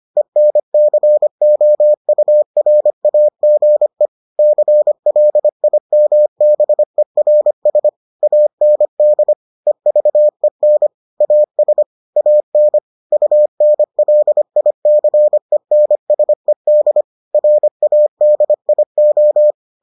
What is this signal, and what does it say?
Morse code, heard as a keyed tone, about 25 wpm, 600 Hz. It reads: ENCOURAGE CLIMBERS AND EVEN AS AN UNLICENSED RADIO